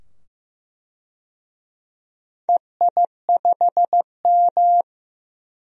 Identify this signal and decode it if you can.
Morse code, heard as a keyed tone, about 15 wpm, 700 Hz. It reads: EI5M
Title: DXCC-100 Call Signs - 4 Characters - 15wpm